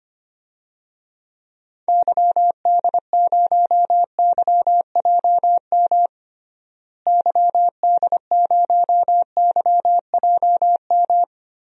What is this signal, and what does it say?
Morse code, heard as a keyed tone, about 25 wpm, 700 Hz. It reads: YD0YJM YD0YJM
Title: DXCC-100 Call Signs - 6 Characters - Repeated 3x - 25wpm